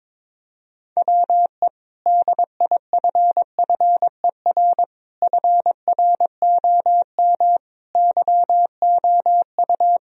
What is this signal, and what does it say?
Morse code, heard as a keyed tone, about 22 wpm, 700 Hz. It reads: WE DIFFER FROM YOU
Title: Sentences from Top 1000 Words - Review Part 1 - 22wpm